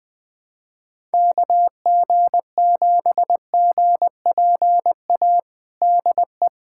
Morse code, heard as a keyed tone, about 20 wpm, 700 Hz. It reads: KG7GPA DE
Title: USA Callsign DE Callsign 20wpm